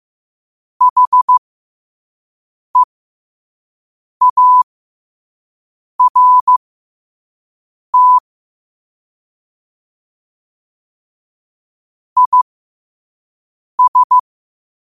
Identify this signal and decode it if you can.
Morse code, heard as a keyed tone, about 15 wpm, spaced as for 5 wpm, 1000 Hz. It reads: HEART IS
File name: qotdfiles / 5wpm